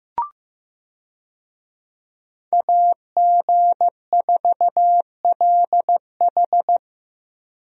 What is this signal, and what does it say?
Morse code, heard as a keyed tone, about 15 wpm, 700 Hz. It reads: AG4LH
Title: USA Call Signs - 5 Characters - Repeated 3x - 15wpm